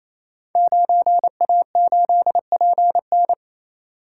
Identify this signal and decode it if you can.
Morse code, heard as a keyed tone, about 27 wpm, 700 Hz. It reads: 9A8PN
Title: Code Groups - 5 Letter:Numbers - 28wpm